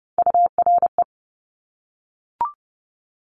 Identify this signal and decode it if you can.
Morse code, heard as a keyed tone, about 30 wpm, 700 Hz. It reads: URE
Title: Top Three Letter Combinations in Words 30wpm